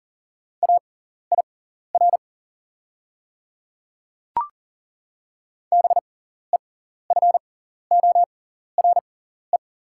Morse code, heard as a keyed tone, about 40 wpm, spaced as for 14 wpm, 690 Hz. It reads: AIR BEFORE
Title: 6x Character Spacing - Top 200 Words - 40wpm